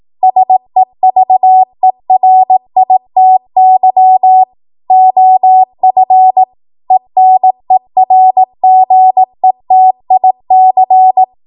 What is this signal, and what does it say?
Morse code, heard as a keyed tone, about 18 wpm, 750 Hz. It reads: SEVERITY OF ENERGETIC
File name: Morse / 18WPM